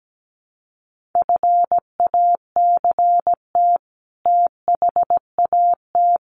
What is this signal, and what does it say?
Morse code, heard as a keyed tone, about 17 wpm, 700 Hz. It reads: FACT THAT
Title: Sets of 2 Words 17wpm